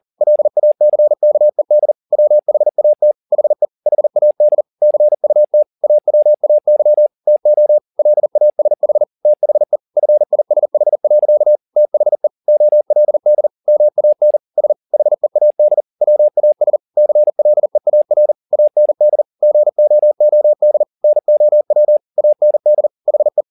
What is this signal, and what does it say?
Morse code, heard as a keyed tone, about 40 wpm, 600 Hz. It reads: LACKED WHAT HE HAD CUT AWAY TO LASH THE FISH. THE OLD MAN S HEAD WAS CLEAR AND GOOD NOW AND HE
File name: the old man and the sea 2 40wpm